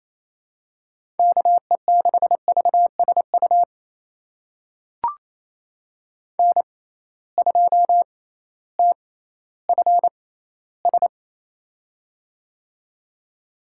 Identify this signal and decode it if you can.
Morse code, heard as a keyed tone, about 28 wpm, spaced as for 15 wpm, 700 Hz. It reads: KE6VSU N2TFS
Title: USA Call Signs - 6x Character Spacing - 28wpm